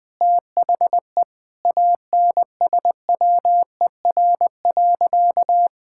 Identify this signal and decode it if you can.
Morse code, heard as a keyed tone, about 20 wpm, 700 Hz. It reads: THE ANSWER.
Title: Sentences with 4-Word Phrases - Spoken Beginning - 20wpm